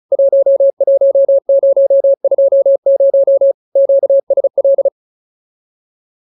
Morse code, heard as a keyed tone, about 34 wpm, 550 Hz. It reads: 11020 QSL